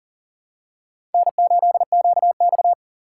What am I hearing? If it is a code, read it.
Morse code, heard as a keyed tone, about 40 wpm, 700 Hz. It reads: N8QX
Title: USA Call Signs - 4 Characters - 6x Character Spacing - 40wpm